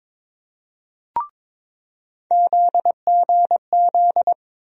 Morse code, heard as a keyed tone, about 21 wpm, 700 Hz. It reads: ZGZ
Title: USA Call Sign Postfixes - 3 Letters 22wpm